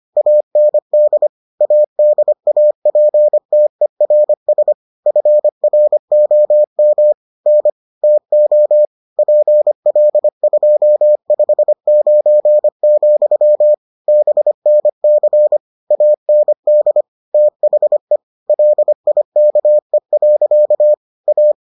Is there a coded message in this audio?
Yes: Morse code from 0.2 to 21.5 s, heard as a keyed tone, about 25 wpm, 600 Hz. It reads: AND ADAPTERS FROM N TO PL259, BNC AND THE LIKE. A